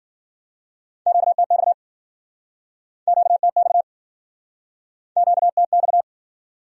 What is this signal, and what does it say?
Morse code, heard as a keyed tone, about 55 wpm, 700 Hz. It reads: QTX QTX QTX